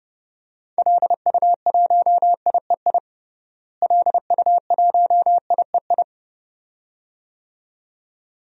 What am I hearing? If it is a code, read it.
Morse code, heard as a keyed tone, about 29 wpm, 700 Hz. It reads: LU1IEI LU1IEI
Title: DXCC-100 Call Signs - 6 Characters - Repeated 2x - 30wpm